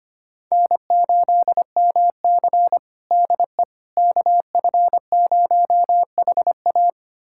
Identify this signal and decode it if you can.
Morse code, heard as a keyed tone, about 25 wpm, 700 Hz. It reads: N8MC DE KF0HA
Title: USA Callsign DE Callsign 25wpm